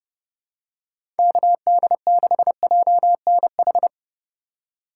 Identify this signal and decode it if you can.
Morse code, heard as a keyed tone, about 30 wpm, 700 Hz. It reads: KD6JNH
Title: USA Call Signs - 6 Characters - 3x Character Spacing - 30wpm